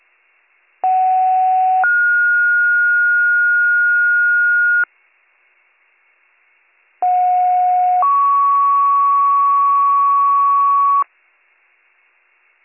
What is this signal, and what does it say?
Two-tone page, heard as a one-second tone, then a three-second tone, about 750 then 1430 Hz, about 730 then 1120 Hz. It is heard from 0.8 to 11.0 s.